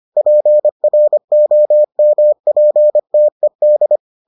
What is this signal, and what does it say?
Morse code, heard as a keyed tone, about 25 wpm, 600 Hz. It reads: PROMPTED